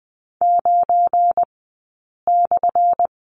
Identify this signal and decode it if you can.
Morse code, heard as a keyed tone, about 20 wpm, spaced as for 8 wpm, 700 Hz. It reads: 9/
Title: 177 Lesson 29d - Slash - Two Characters 20wpm